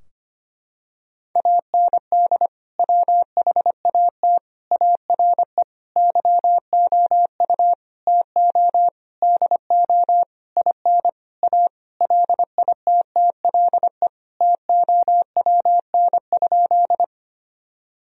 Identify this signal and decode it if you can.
Morse code, heard as a keyed tone, about 25 wpm, 700 Hz. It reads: AND WHAT ARE YOU TO DO IN A LITTLE TOWN?
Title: Sentences from Top 1000 Words - Review Part 2 - 25wpm